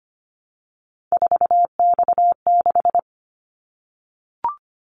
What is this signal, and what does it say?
Morse code, heard as a keyed tone, about 25 wpm, 700 Hz. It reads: 4X6
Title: Callsign Prefixes for Top 50 DXCC - 25wpm